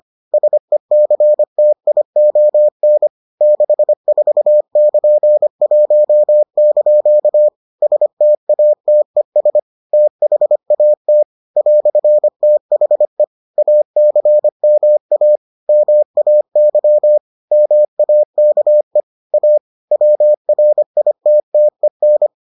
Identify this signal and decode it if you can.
Morse code, heard as a keyed tone, about 25 wpm, 600 Hz. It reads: SECTION 64<KN>1) STATES THAT "THE ACMA MAY MAKE A WRITTEN